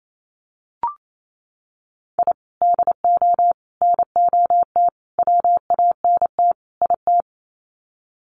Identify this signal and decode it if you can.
Morse code, heard as a keyed tone, about 28 wpm, 700 Hz. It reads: I DO NOT WANT IT
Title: Sets of 5 Words Encore2 28wpm